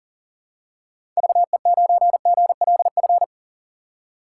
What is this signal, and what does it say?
Morse code, heard as a keyed tone, about 40 wpm, 700 Hz. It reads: VE9GLF
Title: K1USN SST QSO Practice - 40wpm